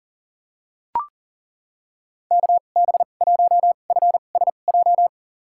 Morse code, heard as a keyed tone, about 40 wpm, spaced as for 29 wpm, 700 Hz. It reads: KB1FSJ
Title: USA Call Signs - 6 Characters - 2x Character Spacing - 40wpm